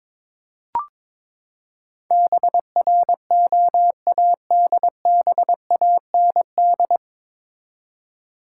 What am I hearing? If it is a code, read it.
Morse code, heard as a keyed tone, about 22 wpm, 700 Hz. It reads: BROADBAND
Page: Morse Code Ninja practice 22wpm